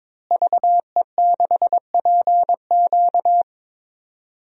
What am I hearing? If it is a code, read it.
Morse code, heard as a keyed tone, about 22 wpm, 700 Hz. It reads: VE6PQ
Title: DXCC-100 Call Signs - 5 Characters - Repeated 3x - 22wpm